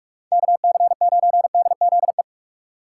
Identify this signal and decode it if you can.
Morse code, heard as a keyed tone, about 44 wpm, 700 Hz. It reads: KC9DZE